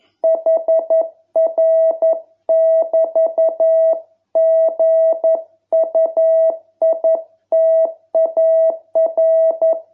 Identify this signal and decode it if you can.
Morse code, heard as a keyed tone, about 11 wpm, 660 Hz. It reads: HR<BT>GUITAR